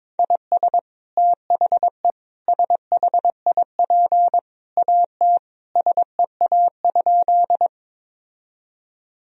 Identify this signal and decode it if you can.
Morse code, heard as a keyed tone, about 22 wpm, 700 Hz. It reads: IS THE SHIP AT SEA?
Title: Sentences from Top 1000 Words - Review Part 2 - 22wpm